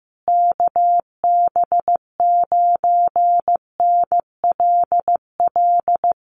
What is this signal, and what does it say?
Morse code, heard as a keyed tone, about 15 wpm, 700 Hz. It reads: KB9NLL
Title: Common QSO Phrases Encore 15wpm